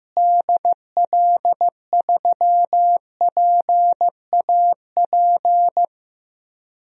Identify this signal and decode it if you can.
Morse code, heard as a keyed tone, about 15 wpm, 700 Hz. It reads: DL3PAP
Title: DXCC-100 Call Signs - 6 Characters - Repeated 2x - 15wpm